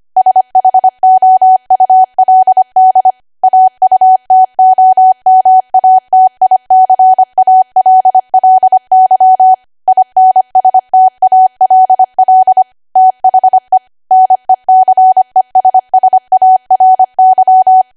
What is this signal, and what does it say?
Morse code, heard as a keyed tone, about 25 wpm, 750 Hz. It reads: SHOULD AUTOMATICALLY INSTALL THE NECESSARY
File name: cwa / W1AW-Level3 / 25WPM